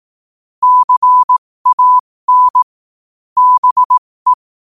Morse code, heard as a keyed tone, about 18 wpm, spaced as for 15 wpm, 1000 Hz. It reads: CAN BE
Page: New quotes every day in morse code at 15 Words per minute.